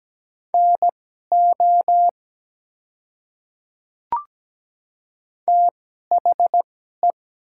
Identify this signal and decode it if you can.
Morse code, heard as a keyed tone, about 17 wpm, spaced as for 12 wpm, 700 Hz. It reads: NO THE
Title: 2x Character Spacing - Top 100 Words - 17wpm